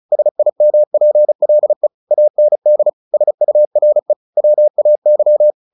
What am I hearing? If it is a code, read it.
Morse code, heard as a keyed tone, about 34 wpm, 600 Hz. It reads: SIMPLE AND SURE WAY
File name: the old man and the sea 2 35wpm